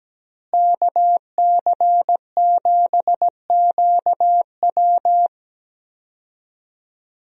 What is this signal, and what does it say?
Morse code, heard as a keyed tone, about 17 wpm, 700 Hz. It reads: KC7QW
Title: USA Call Signs - 5 Characters - Repeated 3x - 17wpm